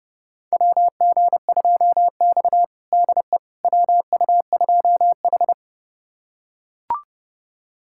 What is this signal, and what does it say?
Morse code, heard as a keyed tone, about 32 wpm, 700 Hz. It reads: WG2X DE WU2H
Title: USA Callsign DE Callsign - 4 Characters - 30wpm